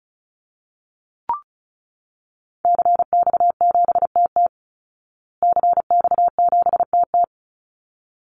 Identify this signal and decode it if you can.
Morse code, heard as a keyed tone, about 34 wpm, 700 Hz. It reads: CX7TT CX7TT
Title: Call Signs Repeated 3x - 35wpm